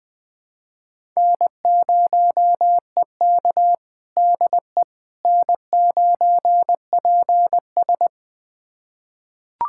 Morse code, heard as a keyed tone, about 20 wpm, 700 Hz. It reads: N0EK DE N9PS
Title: USA Callsign DE Callsign - 4 Characters - 20wpm